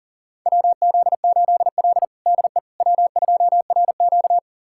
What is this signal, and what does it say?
Morse code, heard as a keyed tone, about 40 wpm, 700 Hz. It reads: WZ8L DE W2RQ
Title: USA Callsign DE Callsign - 4 Characters - 40wpm